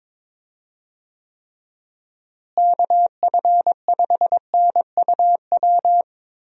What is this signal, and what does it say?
Morse code, heard as a keyed tone, about 21 wpm, 700 Hz. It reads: KF5NUW
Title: USA Call Signs - 6 Characters - Repeated 2x - 22wpm